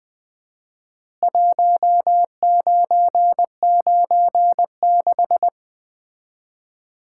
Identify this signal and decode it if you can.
Morse code, heard as a keyed tone, about 20 wpm, 700 Hz. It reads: 1996
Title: Common QSO Phrases - 4x Word-Spacing - 20wpm